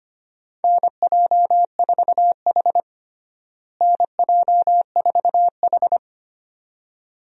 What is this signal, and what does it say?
Morse code, heard as a keyed tone, about 25 wpm, 700 Hz. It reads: NJ4H NJ4H
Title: USA Call Signs - 4 Characters - Repeated 2x - 25wpm